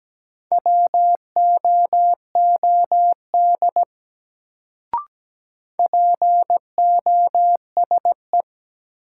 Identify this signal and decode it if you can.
Morse code, heard as a keyed tone, about 17 wpm, 700 Hz. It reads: WOOD POSE
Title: Sentences from Top 1000 Words - Review Part 2 - New Word Drill - 17wpm